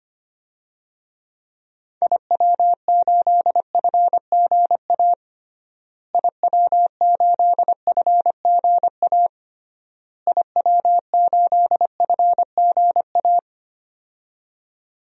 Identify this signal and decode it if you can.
Morse code, heard as a keyed tone, about 25 wpm, 700 Hz. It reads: IW8FGA IW8FGA IW8FGA
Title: DXCC-100 Call Signs - 6 Characters - Repeated 3x - 25wpm